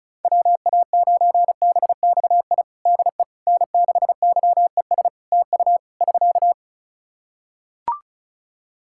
Morse code, heard as a keyed tone, about 36 wpm, 700 Hz. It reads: WA9BXI DE N6YES TU <SK>
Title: Practice QSO - Exchange 4 - Ending - 35wpm